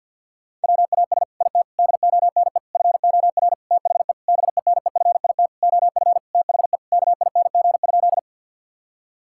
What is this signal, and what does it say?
Morse code, heard as a keyed tone, about 50 wpm, 700 Hz. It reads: WAS IT DONE FOR THE BENEFIT OF THE KING?
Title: Sentences from Top 1000 Words - Review Part 6 - 50wpm